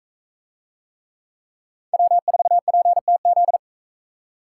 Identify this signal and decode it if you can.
Morse code, heard as a keyed tone, about 38 wpm, 700 Hz. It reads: W4PTZ